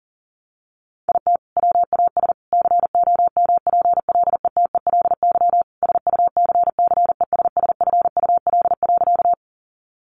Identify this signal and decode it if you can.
Morse code, heard as a keyed tone, about 40 wpm, 700 Hz. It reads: IT WAS COMPLETELY SUCCESSFUL.
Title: Sentences from Top 1100 Words 40wpm